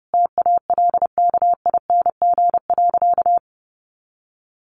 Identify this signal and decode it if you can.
Morse code, heard as a keyed tone, about 30 wpm, 700 Hz. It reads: TALKING.
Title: Additional Easy Phrases 30wpm